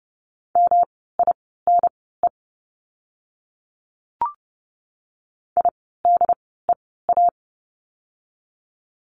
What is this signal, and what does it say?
Morse code, heard as a keyed tone, about 30 wpm, spaced as for 17 wpm, 700 Hz. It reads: MINE IDEA
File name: course / 30:10wpm / 108 Lesson 18f - Letter m - Words 30wpm